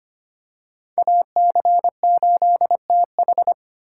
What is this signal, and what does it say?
Morse code, heard as a keyed tone, about 25 wpm, 700 Hz. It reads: AC8TH
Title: USA Call Signs - 5 Characters - 4x Character Spacing - 25wpm